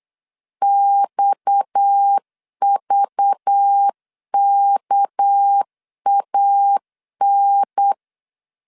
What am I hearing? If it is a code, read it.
Morse code, heard as a keyed tone, about 8 wpm, 800 Hz. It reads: XVKAN
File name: LEZIONI / 6 PKX0 / 6 PKX 8wpm 3041